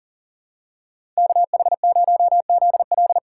Morse code, heard as a keyed tone, about 39 wpm, 700 Hz. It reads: KH0ZL
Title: USA Call Signs - 5 Characters - 2x Character Spacing - 40wpm